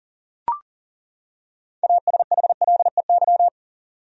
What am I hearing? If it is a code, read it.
Morse code, heard as a keyed tone, about 40 wpm, 700 Hz. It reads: ASHLEY
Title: USA Common Names - 40wpm